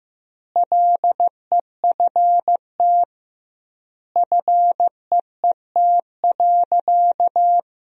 Morse code, heard as a keyed tone, about 15 wpm, 700 Hz. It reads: LEFT FEET.
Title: 2x Word-Spacing - Sentences from Top 300 Words - 15wpm